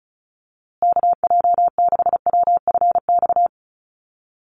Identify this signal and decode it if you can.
Morse code, heard as a keyed tone, about 34 wpm, 700 Hz. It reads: KJ6WFX
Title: USA Call Signs Repeated 3x 35wpm